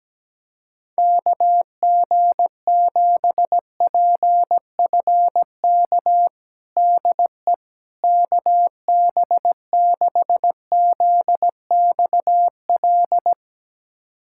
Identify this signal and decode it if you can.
Morse code, heard as a keyed tone, about 17 wpm, 700 Hz. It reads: KG7PFK DE KB6ZXL
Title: USA Callsign DE Callsign - 6 Characters - 17wpm